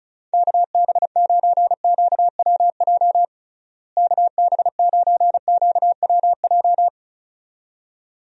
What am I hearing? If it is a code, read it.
Morse code, heard as a keyed tone, about 36 wpm, 700 Hz. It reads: KB9QWJ KB9QWJ
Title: USA Call Signs - 6 Characters - Repeated 2x - 35wpm